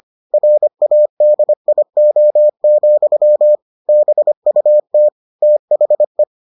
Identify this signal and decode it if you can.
Morse code, heard as a keyed tone, about 25 wpm, 600 Hz. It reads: RADIO, BUT THE